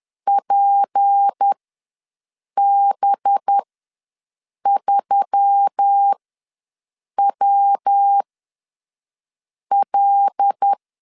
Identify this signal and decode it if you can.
Morse code, heard as a keyed tone, about 10 wpm, spaced as for 5 wpm, 800 Hz. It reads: PB3WL
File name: LEZIONI / 7 GF9Z / 7 G ws 10wpm 3843